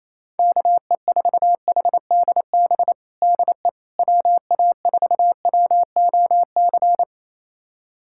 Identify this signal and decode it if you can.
Morse code, heard as a keyed tone, about 28 wpm, 700 Hz. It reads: KE4HDB DE WA4WOC
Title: USA Callsign DE Callsign - 6 Characters - 28wpm